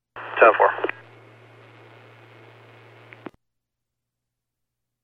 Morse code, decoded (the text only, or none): none